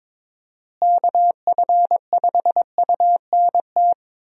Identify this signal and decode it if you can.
Morse code, heard as a keyed tone, about 22 wpm, 700 Hz. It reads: KF5UNT
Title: USA Call Signs - 6 Characters - Repeated 3x - 22wpm